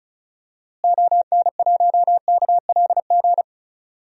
Morse code, heard as a keyed tone, about 36 wpm, 700 Hz. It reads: ON1KLG
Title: DXCC-100 Call Signs - 6 Characters - Repeated 3x - 35wpm